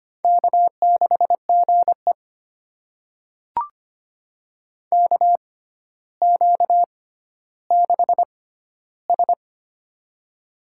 Morse code, heard as a keyed tone, about 25 wpm, spaced as for 13 wpm, 700 Hz. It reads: K6GE KQ6S
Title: USA Call Signs - 6x Character Spacing - 25wpm